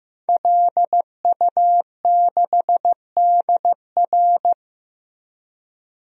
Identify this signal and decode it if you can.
Morse code, heard as a keyed tone, about 15 wpm, 700 Hz. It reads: LU6DR
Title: DXCC-100 Call Signs - 5 Characters - 15wpm